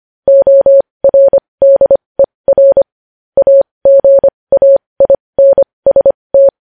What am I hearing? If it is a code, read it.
Morse code, heard as a keyed tone, about 25 wpm, spaced as for 20 wpm, 550 Hz. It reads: ORDER AGAINST
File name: podcast / News headlines-1x-25x20